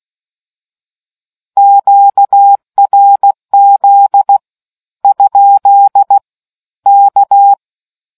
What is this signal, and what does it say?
Morse code, heard as a keyed tone, about 16 wpm, 800 Hz. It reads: QRZ ? K